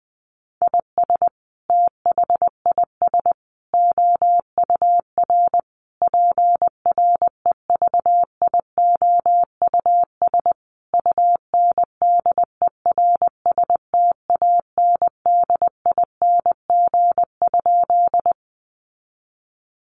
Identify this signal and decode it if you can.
Morse code, heard as a keyed tone, about 20 wpm, 700 Hz. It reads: IS THIS OUR PREVIOUS UNDERSTANDING?
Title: Sentences from Top 1500 Words 20wpm